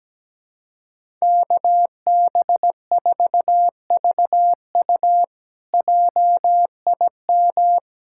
Morse code, heard as a keyed tone, about 17 wpm, 700 Hz. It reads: KB4VU JIM
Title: K1USN SST QSO Practice - 17wpm